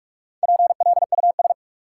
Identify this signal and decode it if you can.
Morse code, heard as a keyed tone, about 46 wpm, 700 Hz. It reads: PLUS
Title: Four Letter Words - 45wpm